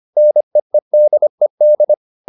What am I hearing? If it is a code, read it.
Morse code, heard as a keyed tone, about 25 wpm, 600 Hz. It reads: NEEDED